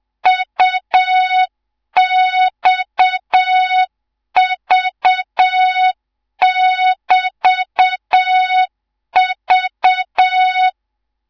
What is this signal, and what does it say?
Morse code, heard as a keyed tone, about 7 wpm, 3700 Hz, with noise. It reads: UXV<BT>V
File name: Morze-HA2MN / hang / 18 IRASJELEK2